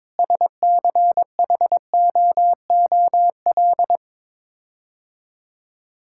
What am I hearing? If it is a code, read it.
Morse code, heard as a keyed tone, about 22 wpm, 700 Hz. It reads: SCHOOL
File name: practice / 22wpm / Sets of 6 Words - 22wpm